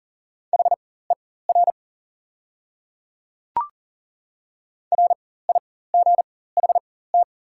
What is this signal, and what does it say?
Morse code, heard as a keyed tone, about 40 wpm, spaced as for 19 wpm, 690 Hz. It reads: HER RIGHT
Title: 4x Character Spacing - Top 500 Words - 40wpm